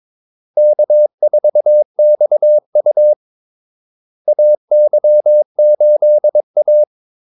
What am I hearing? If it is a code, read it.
Morse code, heard as a keyed tone, about 22 wpm, 600 Hz. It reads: K4XU AY8A
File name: Callsigns-22wpm